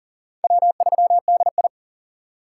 Morse code, heard as a keyed tone, about 40 wpm, 700 Hz. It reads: W3DI
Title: USA Call Signs - 2x Character Spacing - 40wpm